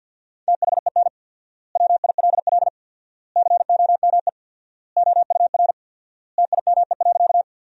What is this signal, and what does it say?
Morse code, heard as a keyed tone, about 50 wpm, 700 Hz. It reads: THEN WILL COME OUR TIME.
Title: 4x Word-Spacing - Sentences from Top 200 Words - 50wpm